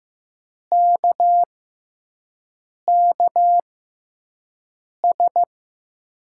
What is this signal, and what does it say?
Morse code, heard as a keyed tone, about 15 wpm, spaced as for 5 wpm, 700 Hz. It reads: KKS
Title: USA Call Signs - 6 Characters - 6x Character Spacing - 15wpm